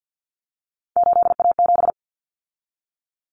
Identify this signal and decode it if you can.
Morse code, heard as a keyed tone, about 50 wpm, 700 Hz. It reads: 8A7
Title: Common Cut Numbers - 3-Digit 50wpm